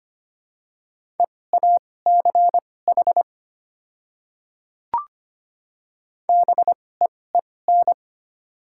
Morse code, heard as a keyed tone, about 25 wpm, spaced as for 18 wpm, 690 Hz. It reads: EACH BEEN
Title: 2x Character Spacing - Top 100 Words - 25wpm